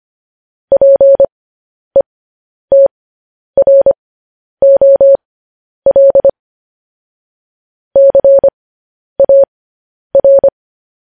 Morse code, heard as a keyed tone, about 25 wpm, spaced as for 10 wpm, 550 Hz. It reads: PETROL CAR
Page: news headlines in morse code at 10 wpm